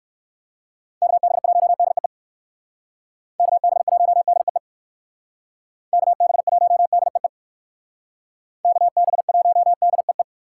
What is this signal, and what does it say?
Morse code, heard as a keyed tone, about 56 wpm, 700 Hz. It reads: KB1DEE KB1DEE KB1DEE KB1DEE
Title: Speed Racing - USA 2x3 Call Signs 45wpm